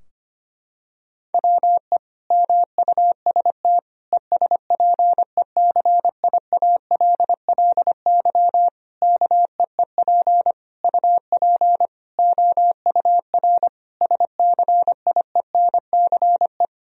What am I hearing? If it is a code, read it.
Morse code, heard as a keyed tone, about 25 wpm, 700 Hz. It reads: WE MUST ESPECIALLY KEEP UP OUR SCIENCE
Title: Sentences from Top 1000 Words - Review Part 4 - 25wpm